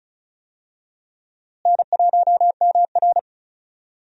Morse code, heard as a keyed tone, about 34 wpm, 700 Hz. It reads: N1MR
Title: USA Call Signs - 4 Characters - Repeated 2x - 35wpm